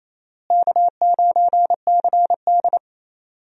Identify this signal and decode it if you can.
Morse code, heard as a keyed tone, about 28 wpm, 700 Hz. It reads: K9CD
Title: USA Call Signs - 4 Characters - Repeated 2x - 28wpm